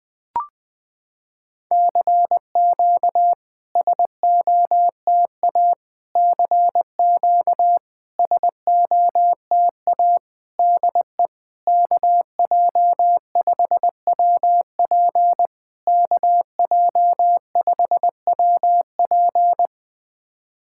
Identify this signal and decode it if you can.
Morse code, heard as a keyed tone, about 20 wpm, 700 Hz. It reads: CQ SOTA CQ SOTA DE KJ5WP KJ5WP
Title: SOTA QSO Practice 20wpm